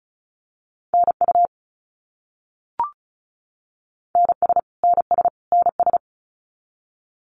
Morse code, heard as a keyed tone, about 34 wpm, 700 Hz. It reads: NU NS NS NS
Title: Canadian Provinces and Territories - Repeated 3x - 35wpm